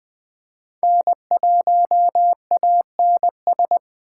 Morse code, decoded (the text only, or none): N1ANS